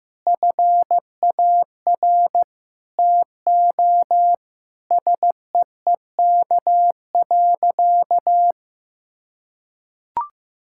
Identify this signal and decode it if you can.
Morse code, heard as a keyed tone, about 15 wpm, 700 Hz. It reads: FAR TO SEEK.
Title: Sentences from Top 1600 Words 15wpm